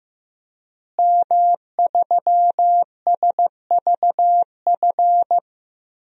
Morse code, heard as a keyed tone, about 15 wpm, 700 Hz. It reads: M3SVF
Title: DXCC-100 Call Signs - 5 Characters - Repeated 2x - 15wpm